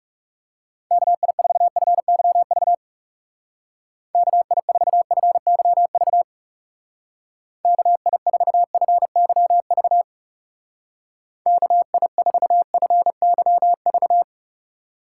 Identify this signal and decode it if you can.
Morse code, heard as a keyed tone, about 37 wpm, 700 Hz. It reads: KI4FYV KI4FYV KI4FYV KI4FYV
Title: Speed Racing - USA Call Signs 30wpm